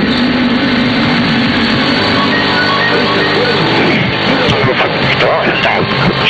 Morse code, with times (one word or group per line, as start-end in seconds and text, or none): none